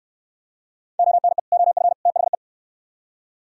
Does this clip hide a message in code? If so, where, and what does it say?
Yes: Morse code from 1.0 to 2.3 s, heard as a keyed tone, about 62 wpm, 700 Hz. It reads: ONE OF THE